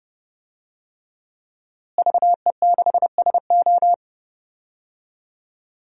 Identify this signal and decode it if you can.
Morse code, heard as a keyed tone, about 30 wpm, 700 Hz. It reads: VE6SO
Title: DXCC-100 Call Signs - 5 Characters - Repeated 3x - 30wpm